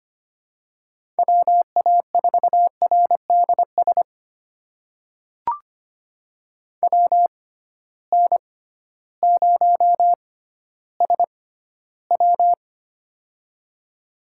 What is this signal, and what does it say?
Morse code, heard as a keyed tone, about 25 wpm, spaced as for 14 wpm, 700 Hz. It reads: WA4RDS WN0SW
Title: USA Call Signs - 6x Character Spacing - 25wpm